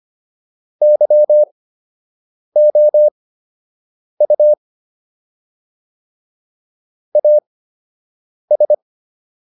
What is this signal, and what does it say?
Morse code, heard as a keyed tone, about 25 wpm, spaced as for 7 wpm, 600 Hz. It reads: YOU AS